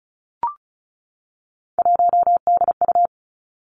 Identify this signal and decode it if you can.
Morse code, heard as a keyed tone, about 36 wpm, 700 Hz. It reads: 1DU
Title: Three Challenging Letter-Number Combinations 35wpm